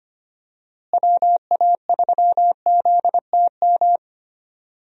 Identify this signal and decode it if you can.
Morse code, heard as a keyed tone, about 25 wpm, 700 Hz. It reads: WA3ZTM
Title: USA Call Signs - 6 Characters - Repeated 2x - 25wpm